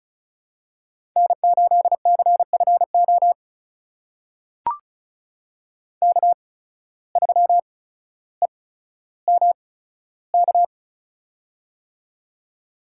Morse code, heard as a keyed tone, about 34 wpm, spaced as for 15 wpm, 700 Hz. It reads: N8CFO K3EMK
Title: USA Call Signs - 5 Characters - 8x Character Spacing - 35wpm